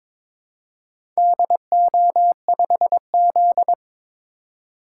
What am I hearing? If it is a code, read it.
Morse code, heard as a keyed tone, about 22 wpm, 700 Hz. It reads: DO5Z
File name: practice / 22wpm / Four Letter-Number Combinations - 22wpm